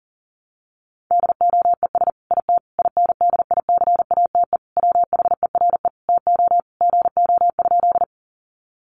Morse code, heard as a keyed tone, about 40 wpm, 700 Hz. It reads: DOES IT INDICATE WHERE TO GO?
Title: Sentences from Top 1300 Words 40wpm